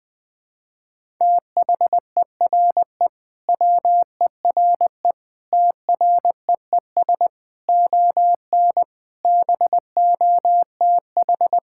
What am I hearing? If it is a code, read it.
Morse code, heard as a keyed tone, about 21 wpm, 700 Hz. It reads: THERE WERE TREES ON BOTH
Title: Sentences with 5-Word Phrases - Spoken Beginning - 20wpm